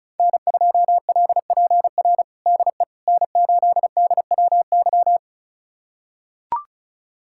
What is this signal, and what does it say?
Morse code, heard as a keyed tone, about 36 wpm, 700 Hz. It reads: N2LPR DE N8DWY
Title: USA Callsign DE Callsign - 5 Characters - 35wpm